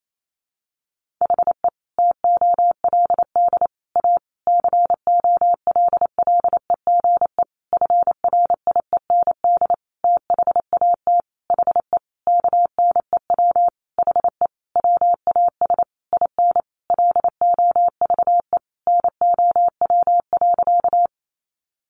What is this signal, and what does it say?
Morse code, heard as a keyed tone, about 28 wpm, 700 Hz. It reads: HE TOLD A COLLEGE FRIEND THAT HE KNEW HE WAS IN LOVE NOW.
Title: Sentences from Top 1300 Words 28wpm